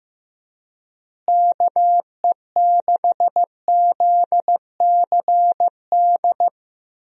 Morse code, heard as a keyed tone, about 15 wpm, 700 Hz. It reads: KE6ZCD
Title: USA Call Signs - 6 Characters - Repeated 2x - 15wpm